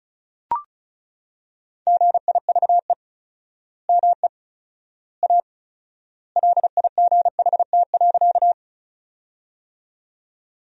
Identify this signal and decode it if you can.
Morse code, heard as a keyed tone, about 34 wpm, 700 Hz. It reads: GIVE ME A LIGHT.
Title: 4x Word-Spacing - Sentences from Top 200 Words - 35wpm